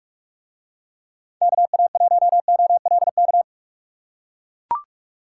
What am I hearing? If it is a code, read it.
Morse code, heard as a keyed tone, about 41 wpm, 700 Hz. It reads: KA1OLK